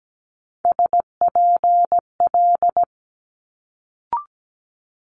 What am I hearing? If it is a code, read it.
Morse code, heard as a keyed tone, about 17 wpm, 700 Hz. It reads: SPL
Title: Multi-letter Phonemes Quick Fire 17wpm